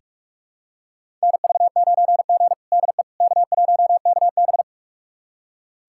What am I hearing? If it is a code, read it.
Morse code, heard as a keyed tone, about 46 wpm, 700 Hz. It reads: NV9G DE K1KB
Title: USA Callsign DE Callsign - 4 Characters - 45wpm